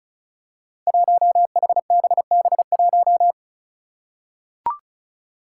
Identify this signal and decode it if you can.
Morse code, heard as a keyed tone, about 36 wpm, 700 Hz. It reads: 1HBB1
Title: Five Challenging Letter-Number Combinations 35wpm